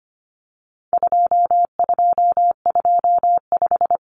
Morse code, heard as a keyed tone, about 25 wpm, 700 Hz. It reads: 2225
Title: Common Cut Numbers - 4-Digit 25wpm